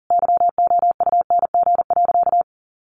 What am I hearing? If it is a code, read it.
Morse code, heard as a keyed tone, about 40 wpm, 700 Hz. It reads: YOUNG.
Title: Sentences from Top 400 Words 40wpm